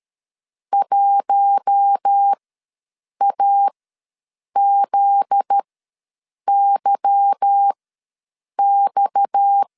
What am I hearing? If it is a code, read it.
Morse code, heard as a keyed tone, about 13 wpm, spaced as for 7 wpm, 800 Hz. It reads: 1AZYX
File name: LEZIONI / 9 28 ws 12wpm 3354